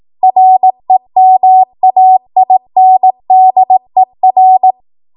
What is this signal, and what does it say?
Morse code, heard as a keyed tone, about 18 wpm, 750 Hz. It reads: REMAINDER
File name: Archive / 18 WPM / 18WPM